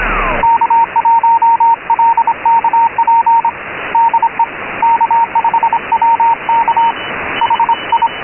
Morse code, heard as a keyed tone, about 25 wpm, 920 Hz, with radio noise. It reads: K1LKP DE K5WK HI
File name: Radio / SDR Audio / gqrx